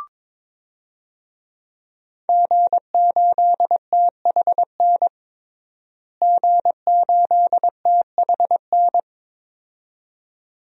Morse code, heard as a keyed tone, about 22 wpm, 700 Hz. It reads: G8THN G8THN
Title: DXCC-100 Call Signs - 5 Characters - Repeated 2x - 22wpm